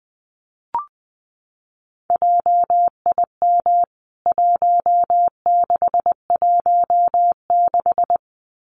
Morse code, heard as a keyed tone, about 20 wpm, 700 Hz. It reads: JIM 1616
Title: CWT Exchange 20wpm